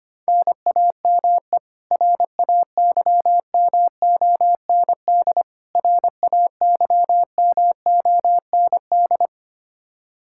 Morse code, heard as a keyed tone, about 25 wpm, 700 Hz. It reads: NAME RAYMOND RAYMOND
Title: Practice QSO - Exchange 1 - Beginning - 25wpm